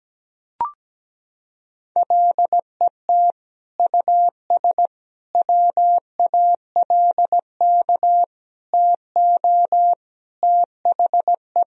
Morse code, heard as a keyed tone, about 17 wpm, 700 Hz. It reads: LET US WALK TO THE
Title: Sentences from Top 1000 Words - Review Part 2 - 17wpm